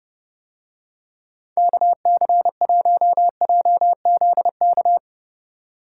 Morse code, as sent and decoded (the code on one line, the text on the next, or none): -.- -.-. .---- .--- --.. -.-
KC1JZK